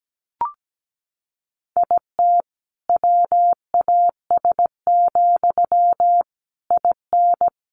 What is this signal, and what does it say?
Morse code, heard as a keyed tone, about 17 wpm, 700 Hz. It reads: IT WAS, IN
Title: Sentences from Top 1400 Words 17wpm